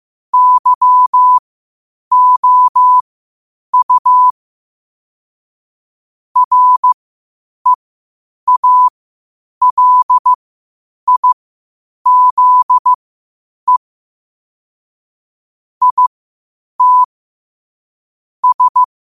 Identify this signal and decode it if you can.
Morse code, heard as a keyed tone, about 15 wpm, spaced as for 8 wpm, 1000 Hz. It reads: YOU REALIZE IT S
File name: qotdfiles / 8wpm